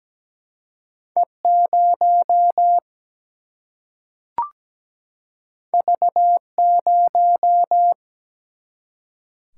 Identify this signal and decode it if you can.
Morse code, heard as a keyed tone, about 17 wpm, 700 Hz. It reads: E 0 V 0
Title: Single-Letter Single-Number 17wpm